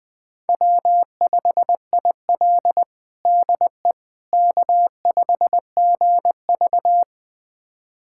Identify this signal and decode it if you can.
Morse code, heard as a keyed tone, about 20 wpm, 700 Hz. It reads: W5IL DE K5GV
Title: USA Callsign DE Callsign - 4 Characters - 20wpm